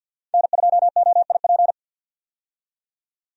Morse code, heard as a keyed tone, about 51 wpm, 700 Hz. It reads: N2OIP
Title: USA Call Signs - 5 Characters - 50wpm